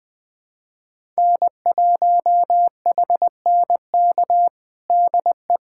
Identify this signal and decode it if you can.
Morse code, heard as a keyed tone, about 20 wpm, 700 Hz. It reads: N1HNK DE
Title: USA Callsign DE Callsign - 5 Characters - 20wpm